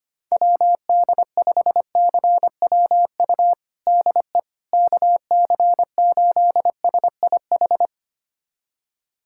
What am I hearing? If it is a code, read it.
Morse code, heard as a keyed tone, about 25 wpm, 700 Hz. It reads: WD5CWU DE KC8SIH
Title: USA Callsign DE Callsign 25wpm